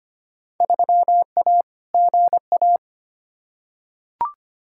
Morse code, heard as a keyed tone, about 25 wpm, 700 Hz. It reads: 3A GA
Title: Field Day Exchanges - 25wpm